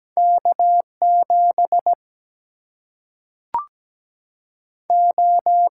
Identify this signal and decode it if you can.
Morse code, heard as a keyed tone, about 17 wpm, 700 Hz. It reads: K7 O
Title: Single-Letter Single-Number 17wpm